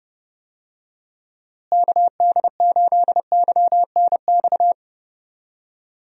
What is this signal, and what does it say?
Morse code, heard as a keyed tone, about 29 wpm, 700 Hz. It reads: KD8YNX
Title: USA Call Signs - 6 Characters - Repeated 2x - 30wpm